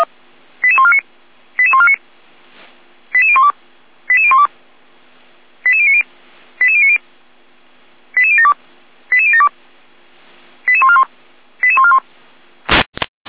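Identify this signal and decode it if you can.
Five-tone selective call (ZVEI), heard as quick runs of five tones, tones of about 70 ms. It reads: 88149 88149 88012 88012 88090 88090 88072 88072 88141 88141
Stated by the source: ZVEI five-tone call 88501 received by SDR.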